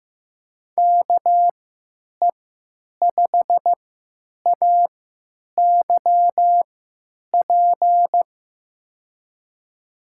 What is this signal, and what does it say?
Morse code, heard as a keyed tone, about 15 wpm, spaced as for 9 wpm, 700 Hz. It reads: KE5AYP
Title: USA Call Signs - 6 Characters - 3x Character Spacing - 15wpm